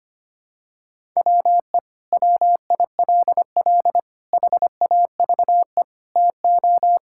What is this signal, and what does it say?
Morse code, heard as a keyed tone, about 25 wpm, 700 Hz. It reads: WE WILL HAVE TO
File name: practice / 25wpm / Sentences with 3-Word Phrases - 25wpm